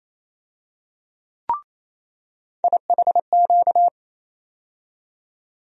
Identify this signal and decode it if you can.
Morse code, heard as a keyed tone, about 27 wpm, 700 Hz. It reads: IHQ
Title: USA Call Sign Postfixes - 3 Letters 28wpm